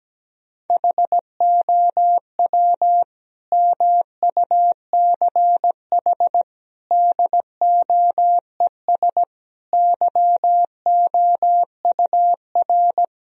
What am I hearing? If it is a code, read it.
Morse code, heard as a keyed tone, about 17 wpm, 700 Hz. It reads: HOW MUCH DOES YOUR
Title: Sentences from Top 1000 Words - Review Part 4 - 17wpm